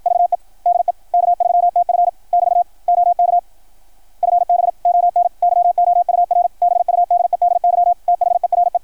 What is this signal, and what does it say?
Morse code, heard as a keyed tone, about 53 wpm, 690 Hz, with radio noise. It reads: VE DE K3TF <BT> OK FB ON YOUR BUDEDY THERE
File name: archived / CRSnet